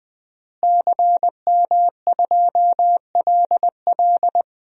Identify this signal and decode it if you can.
Morse code, heard as a keyed tone, about 20 wpm, 700 Hz. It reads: CM2LL
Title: DXCC-100 Call Signs - 5 Characters - Repeated 2x - 20wpm